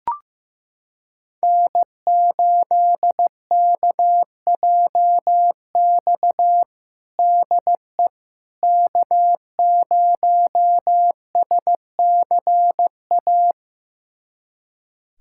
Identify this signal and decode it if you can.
Morse code, heard as a keyed tone, about 15 wpm, 700 Hz. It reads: N8KJX DE K0SCA
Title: USA Callsign DE Callsign - 5 Characters - 15wpm